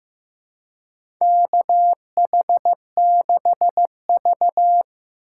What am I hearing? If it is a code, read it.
Morse code, heard as a keyed tone, about 15 wpm, 700 Hz. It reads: KH6V
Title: USA Call Signs - 4 Characters - 3x Character Spacing - 15wpm